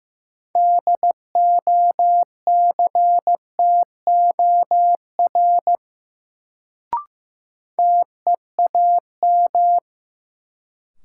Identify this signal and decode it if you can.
Morse code, heard as a keyed tone, about 16 wpm, 700 Hz. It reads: DOCTOR TEAM
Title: Sentences from Top 1500 Words - New Word Drill Alt - 15wpm